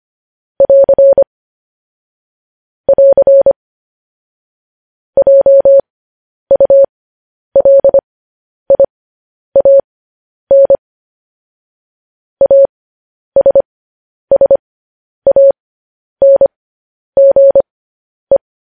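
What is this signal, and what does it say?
Morse code, heard as a keyed tone, about 25 wpm, spaced as for 10 wpm, 550 Hz. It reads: <AR> <AR> JULIAN ASSANGE
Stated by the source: Morse code news headlines practice transmission.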